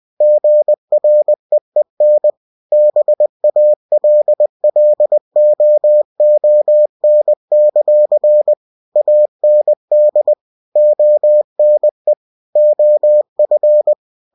Morse code, heard as a keyed tone, about 20 wpm, 600 Hz. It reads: GREEN BALLOON; AND ONE OF
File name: cw-books / winnie-the-pooh / winnie-20wpm-page-0005